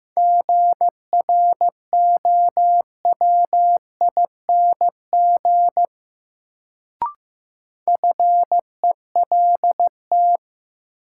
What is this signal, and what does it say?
Morse code, heard as a keyed tone, about 15 wpm, 700 Hz. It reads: GROWING FELT
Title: Sets of 5 Words Encore - New Word Drill - 15wpm